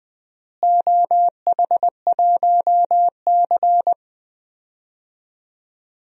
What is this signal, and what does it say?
Morse code, heard as a keyed tone, about 20 wpm, 700 Hz. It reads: OH1C
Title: DXCC-100 Call Signs - 4 Characters - Repeated 3x - 20wpm